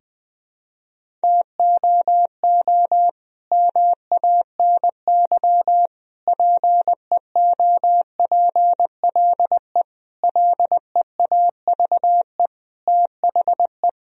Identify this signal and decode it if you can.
Morse code, heard as a keyed tone, about 20 wpm, 700 Hz. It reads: TOO MANY PEOPLE LEAVE THE
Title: Sentences from Top 1000 Words - Review Part 1 - 20wpm